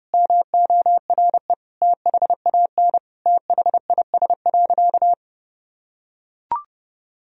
Morse code, heard as a keyed tone, about 32 wpm, 700 Hz. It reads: MORE THAN THIS.
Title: Sentences from Top 1000 Words - Review Part 1 - 30wpm